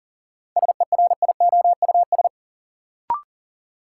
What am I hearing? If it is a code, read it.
Morse code, heard as a keyed tone, about 40 wpm, 700 Hz. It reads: SERIOUS
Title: Sentences from Top 1100 Words - New Word Drill Alt - 40wpm